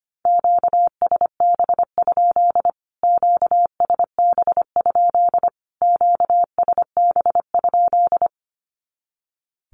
Morse code, heard as a keyed tone, about 25 wpm, 700 Hz. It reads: QSB? QSB? QSB?
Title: Q-Codes Repeated 3x 25wpm